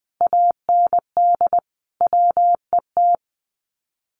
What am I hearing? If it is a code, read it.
Morse code, heard as a keyed tone, about 20 wpm, 700 Hz. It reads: AND WET
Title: Common QSO Phrases Encore 20wpm